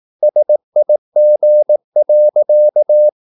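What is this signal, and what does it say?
Morse code, heard as a keyed tone, about 18 wpm, 600 Hz. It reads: SIG.
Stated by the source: Slow (QRS) Morse practice at 17 wpm, from CW med Gnister.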